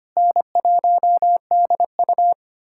Morse code, heard as a keyed tone, about 25 wpm, 700 Hz. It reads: N1DU
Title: USA Callsign DE Callsign - 4 Characters - 25wpm